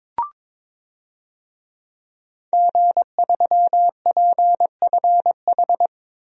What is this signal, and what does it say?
Morse code, heard as a keyed tone, about 22 wpm, 700 Hz. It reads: G3PFH
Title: DXCC-100 Call Signs - 5 Characters - Repeated 3x - 22wpm